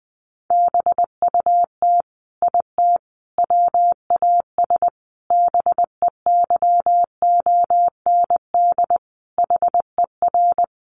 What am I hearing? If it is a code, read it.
Morse code, heard as a keyed tone, about 19 wpm, 700 Hz. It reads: BUT IT WAS BEYOND HER
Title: Sentences from Top 1200 Words 20wpm